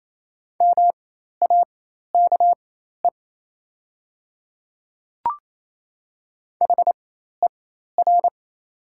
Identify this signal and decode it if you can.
Morse code, heard as a keyed tone, about 28 wpm, spaced as for 13 wpm, 700 Hz. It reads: MAKE HER
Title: 4x Character Spacing - Top 100 Words - 28wpm